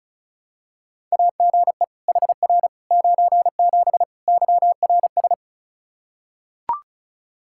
Morse code, heard as a keyed tone, about 36 wpm, 700 Hz. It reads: AGE HR 97 YRS
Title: Practice QSO - Exchange 3 - More Info - 35wpm